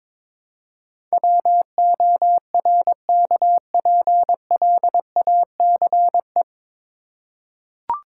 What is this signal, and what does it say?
Morse code, heard as a keyed tone, about 21 wpm, 700 Hz. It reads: WORKPLACE
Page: Morse Code Ninja practice 22wpm